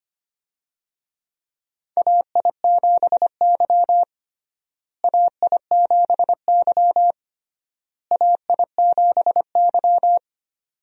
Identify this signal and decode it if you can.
Morse code, heard as a keyed tone, about 25 wpm, 700 Hz. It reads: AI7Y AI7Y AI7Y
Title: USA Call Signs - 4 Characters - Repeated 3x - 25wpm